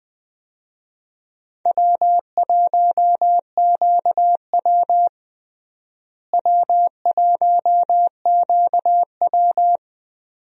Morse code, heard as a keyed tone, about 20 wpm, 700 Hz. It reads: W1QW W1QW
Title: USA Call Signs - 4 Characters - Repeated 2x - 20wpm